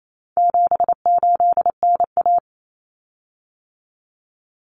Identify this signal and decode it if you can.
Morse code, heard as a keyed tone, about 28 wpm, 700 Hz. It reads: 78NA
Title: Common Cut Numbers - 4-Digit 28wpm